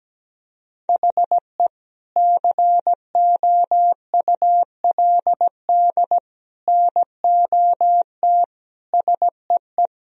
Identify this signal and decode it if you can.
Morse code, heard as a keyed tone, about 17 wpm, 700 Hz. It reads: HE COULD NOT SEE
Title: Sentences from Top 1000 Words - Review Part 2 - 17wpm